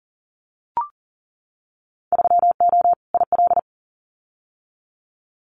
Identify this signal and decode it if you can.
Morse code, heard as a keyed tone, about 41 wpm, 700 Hz. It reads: 3O IL
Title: Winter Field Day Exchanges 40wpm